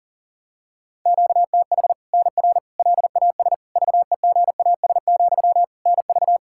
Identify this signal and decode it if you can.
Morse code, heard as a keyed tone, about 40 wpm, 700 Hz. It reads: QTH NR LAS VEGAS, NV
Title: Practice QSO - Exchange 1 - Beginning - 40wpm